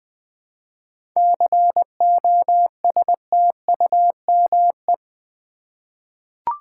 Morse code, heard as a keyed tone, about 20 wpm, 700 Hz. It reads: COSTUME
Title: 7 Letter Words - First 3 Letters Given - 20wpm